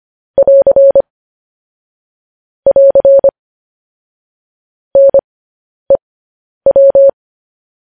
Morse code, heard as a keyed tone, about 25 wpm, spaced as for 10 wpm, 550 Hz. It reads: <AR> <AR> NEW
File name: podcast / News headlines-1x-25x10